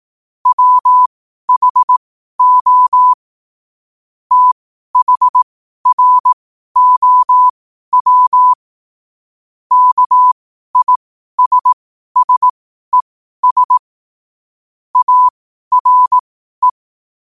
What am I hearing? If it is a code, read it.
Morse code, heard as a keyed tone, about 18 wpm, spaced as for 12 wpm, 1000 Hz. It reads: WHO THROW KISSES ARE
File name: qotdfiles / 12wpm